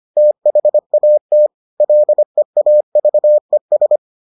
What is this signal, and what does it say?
Morse code, heard as a keyed tone, about 25 wpm, 600 Hz. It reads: THAT LEAVES